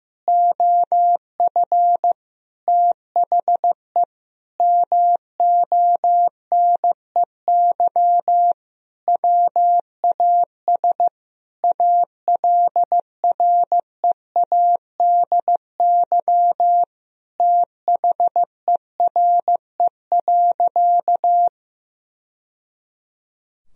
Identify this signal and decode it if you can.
Morse code, heard as a keyed tone, about 15 wpm, 700 Hz. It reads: OF THE MONEY WAS ALREADY THERE.
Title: Sentences from Top 1000 Words - Review Part 1 - 15wpm